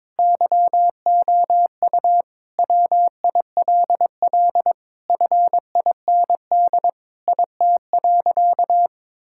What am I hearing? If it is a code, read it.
Morse code, heard as a keyed tone, about 22 wpm, 700 Hz. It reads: YOU WILL FIND IT.
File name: practice / 22wpm / Sentences with 3-Word Phrases - 22wpm